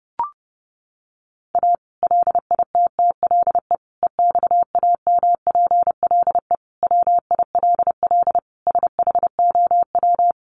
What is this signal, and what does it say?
Morse code, heard as a keyed tone, about 30 wpm, 700 Hz. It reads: A LITTLE EXAMPLE WILL SHOW
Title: Sentences from Top 1500 Words 30wpm